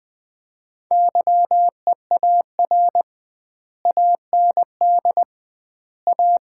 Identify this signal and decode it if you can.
Morse code, heard as a keyed tone, about 21 wpm, 700 Hz. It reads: YEAR AND A
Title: 2x Word-Spacing - Sets of 4 Words - 20wpm